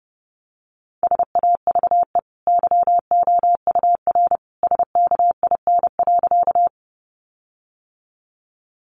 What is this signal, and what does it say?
Morse code, heard as a keyed tone, about 30 wpm, 700 Hz. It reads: SAVE YOUR SKIN.
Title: Sentences from Top 1300 Words 30wpm